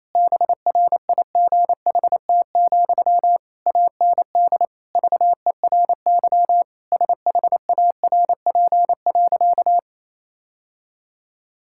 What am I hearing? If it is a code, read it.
Morse code, heard as a keyed tone, about 28 wpm, 700 Hz. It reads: BRIGHT, AND VERY SHARP.
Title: Sentences from Top 1600 Words 28wpm